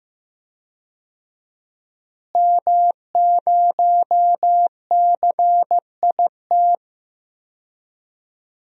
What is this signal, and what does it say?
Morse code, heard as a keyed tone, about 15 wpm, 700 Hz. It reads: M0CIT
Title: DXCC-100 Call Signs - 5 Characters - Repeated 3x - 15wpm